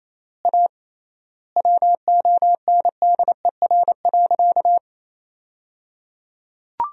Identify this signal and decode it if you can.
Morse code, heard as a keyed tone, about 28 wpm, 700 Hz. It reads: A WONDER.
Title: 3x Word-Spacing - Sentences from Top 500 Words - 28wpm